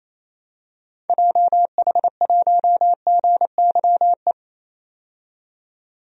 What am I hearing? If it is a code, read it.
Morse code, heard as a keyed tone, about 28 wpm, 700 Hz. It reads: JH1GYE
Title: DXCC-100 Call Signs - 6 Characters - 28wpm